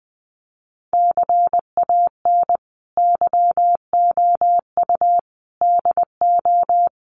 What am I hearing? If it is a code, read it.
Morse code, heard as a keyed tone, about 20 wpm, 700 Hz. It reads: CAN YOU DO
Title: Sentences from Top 100 Words 20wpm